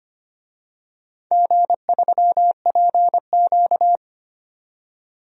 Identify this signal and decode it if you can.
Morse code, heard as a keyed tone, about 25 wpm, 700 Hz. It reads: G3PQ
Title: DXCC-100 Call Signs - 4 Characters - 25wpm